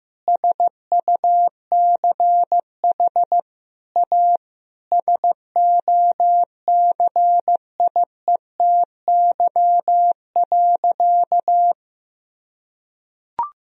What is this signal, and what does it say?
Morse code, heard as a keyed tone, about 15 wpm, 700 Hz. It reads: SUCH A SOCIETY.
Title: Sentences from Top 1000 Words - Review Part 4 - 15wpm